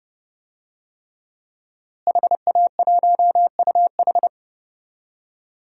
Morse code, heard as a keyed tone, about 30 wpm, 700 Hz. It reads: HA1UH
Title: DXCC-100 Call Signs - 5 Characters - Repeated 3x - 30wpm